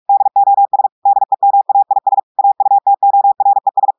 Morse code, heard as a keyed tone, about 46 wpm, 800 Hz. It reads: DOS DEMAIS AUTORES
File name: portugais texte en clair 1 vitesse 45 mots minutes